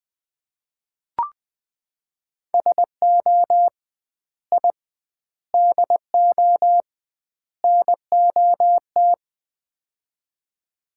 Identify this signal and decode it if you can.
Morse code, heard as a keyed tone, about 20 wpm, 700 Hz. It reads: SO I DO NOT
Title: 2x Word-Spacing - Sets of 4 Words - 20wpm